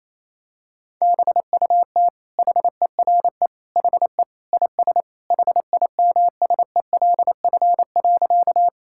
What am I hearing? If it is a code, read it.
Morse code, heard as a keyed tone, about 28 wpm, 700 Hz. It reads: BUT HERE HE IS HIMSELF.
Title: Sentences from Top 1600 Words 28wpm